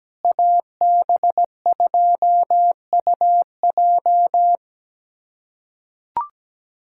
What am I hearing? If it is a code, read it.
Morse code, heard as a keyed tone, about 17 wpm, 700 Hz. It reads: AB2UJ
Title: USA Callsign DE Callsign - 5 Characters - 17wpm